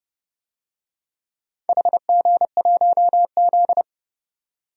Morse code, heard as a keyed tone, about 30 wpm, 700 Hz. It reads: HG1Z
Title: DXCC-100 Call Signs - 4 Characters - 30wpm